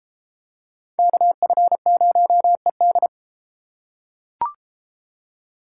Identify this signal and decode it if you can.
Morse code, heard as a keyed tone, about 29 wpm, 700 Hz. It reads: KF0ED